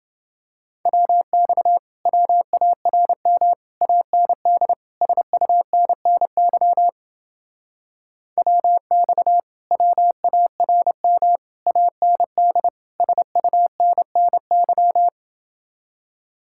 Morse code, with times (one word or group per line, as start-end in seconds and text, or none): 0.9-1.8 WX
2.1-3.5 WARM
3.8-4.7 AND
5.0-6.9 SUNNY
8.4-9.4 WX
9.7-11.4 WARM
11.7-12.7 AND
13.0-15.1 SUNNY